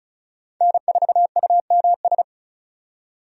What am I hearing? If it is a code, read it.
Morse code, heard as a keyed tone, about 36 wpm, 700 Hz. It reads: N4UMS
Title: USA Call Signs - 5 Characters - 35wpm